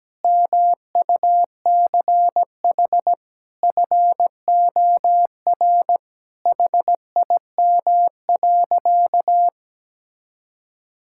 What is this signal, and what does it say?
Morse code, heard as a keyed tone, about 17 wpm, 700 Hz. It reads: MUCH FOR HIM.
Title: Sentences from Top 1000 Words - Review Part 5 - 17wpm